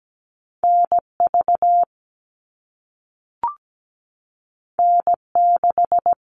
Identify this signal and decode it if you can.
Morse code, heard as a keyed tone, about 17 wpm, 700 Hz. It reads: NV N6
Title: All Cut Numbers - 2-Digit 17wpm